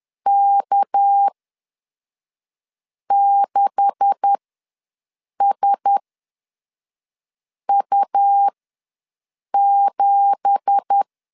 Morse code, heard as a keyed tone, about 10 wpm, spaced as for 5 wpm, 800 Hz. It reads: K6SU7
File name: LEZIONI / 9 2867 ws 10wpm 4215